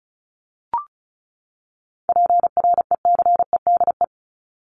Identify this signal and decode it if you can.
Morse code, heard as a keyed tone, about 34 wpm, 700 Hz. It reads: PRECEDE
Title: ING Suffix - Top 100 Words with Base Words 35wpm